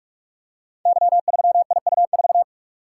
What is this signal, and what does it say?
Morse code, heard as a keyed tone, about 44 wpm, 700 Hz. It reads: Y3IU4
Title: Code Groups - 5 Letter:Numbers - 45wpm